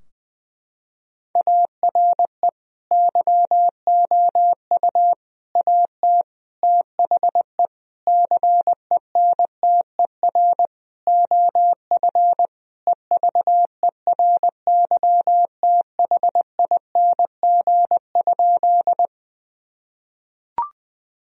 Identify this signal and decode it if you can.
Morse code, heard as a keyed tone, about 19 wpm, 700 Hz. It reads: ARE YOU AT THE CENTER OF EVERYTHING?
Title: Sentences from Top 1000 Words - Review Part 2 - 20wpm